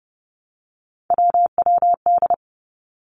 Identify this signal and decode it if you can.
Morse code, heard as a keyed tone, about 30 wpm, 700 Hz. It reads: WWD
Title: Three Challenging Letter-Number Combinations 30wpm